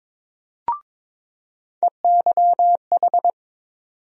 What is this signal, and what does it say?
Morse code, heard as a keyed tone, about 22 wpm, 700 Hz. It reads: EYH
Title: USA Call Sign Postfixes - 3 Letters 22wpm